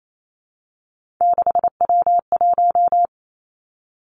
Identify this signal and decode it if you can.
Morse code, heard as a keyed tone, about 28 wpm, 700 Hz. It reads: 6W1
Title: Three Challenging Letter-Number Combinations 28wpm